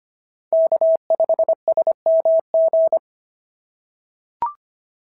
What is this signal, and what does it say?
Morse code, heard as a keyed tone, about 25 wpm, 690 Hz. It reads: K5SMG
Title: SOTA QSO Practice 25wpm